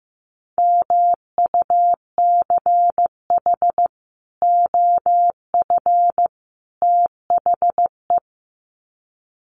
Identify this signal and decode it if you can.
Morse code, heard as a keyed tone, about 15 wpm, 700 Hz. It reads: MUCH OF THE
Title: Sets of 3 Words 15wpm